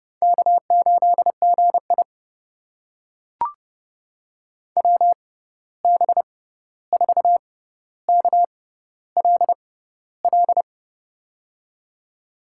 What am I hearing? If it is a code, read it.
Morse code, heard as a keyed tone, about 30 wpm, spaced as for 14 wpm, 700 Hz. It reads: K8GI WB4KLL
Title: USA Call Signs - 6x Character Spacing - 30wpm